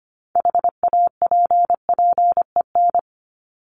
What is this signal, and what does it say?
Morse code, heard as a keyed tone, about 25 wpm, 700 Hz. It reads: HAPPEN
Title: ED Suffix - Top 100 Words with Base Words 25wpm